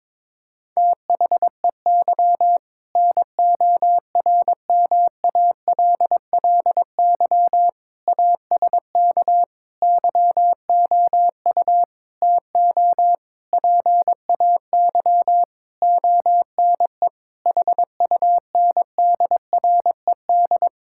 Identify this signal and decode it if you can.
Morse code, heard as a keyed tone, about 22 wpm, 700 Hz. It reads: THEY NORMALLY ASK YOU TO PAY ONE HUNDRED
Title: Sentences from Top 1000 Words - Review Part 6 - 22wpm